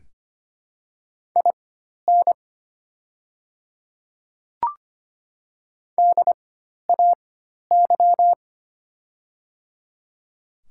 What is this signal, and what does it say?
Morse code, heard as a keyed tone, about 25 wpm, spaced as for 12 wpm, 700 Hz. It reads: IN DAY
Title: 4x Character Spacing - Top 100 Words - 25wpm